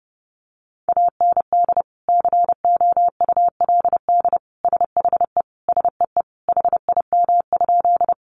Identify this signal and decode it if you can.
Morse code, heard as a keyed tone, about 30 wpm, 700 Hz. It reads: AND COULD SHE SEE HIM?
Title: Sentences from Top 100 Words 30wpm